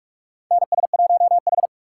Morse code, heard as a keyed tone, about 46 wpm, 690 Hz. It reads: NS1H
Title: USA Call Signs - 4 Characters - 2x Character Spacing - 45wpm